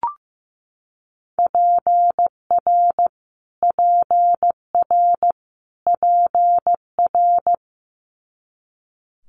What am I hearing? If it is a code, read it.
Morse code, heard as a keyed tone, about 15 wpm, 700 Hz. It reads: PR PR PR
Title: Multi-letter Phonemes Repeated 3x 15wpm